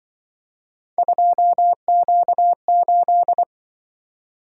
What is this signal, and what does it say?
Morse code, heard as a keyed tone, about 25 wpm, 700 Hz. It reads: 2Q8
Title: Speed-Racing - Three Letter-Number Combinations - 20wpm